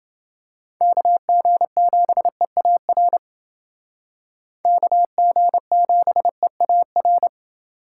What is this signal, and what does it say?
Morse code, heard as a keyed tone, about 28 wpm, 700 Hz. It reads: KG7EAR KG7EAR